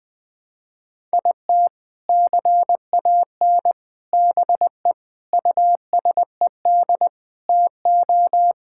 Morse code, heard as a keyed tone, about 21 wpm, 700 Hz. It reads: IT CAN BE USED TO
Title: Sentences with 4-Word Phrases - Spoken Beginning - 20wpm